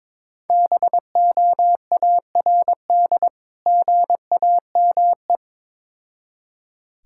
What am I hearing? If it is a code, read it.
Morse code, heard as a keyed tone, about 22 wpm, 700 Hz. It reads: BOARD GAME
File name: Difficult Words with Hints - 22wpm